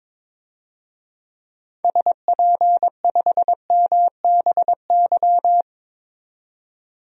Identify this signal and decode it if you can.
Morse code, heard as a keyed tone, about 22 wpm, 700 Hz. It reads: SP5MBY
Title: DXCC-100 Call Signs - 6 Characters - 22wpm